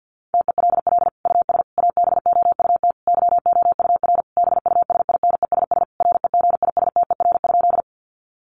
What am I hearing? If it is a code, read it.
Morse code, heard as a keyed tone, about 50 wpm, 700 Hz. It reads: TELL US ABOUT YOUR BUSINESS REGISTER?
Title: Sentences from Top 1200 Words 50wpm